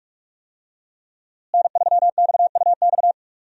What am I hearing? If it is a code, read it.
Morse code, heard as a keyed tone, about 46 wpm, 700 Hz. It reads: N3XUX
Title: USA Call Signs - 5 Characters - Repeated 2x - 45wpm